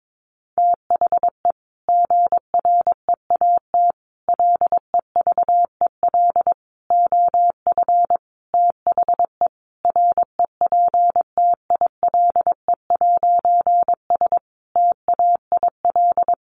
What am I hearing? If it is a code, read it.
Morse code, heard as a keyed tone, about 22 wpm, 700 Hz. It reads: THE GREAT LEVEL OF THE REPTILE'S TAIL
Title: A Sound Of Thunder 22wpm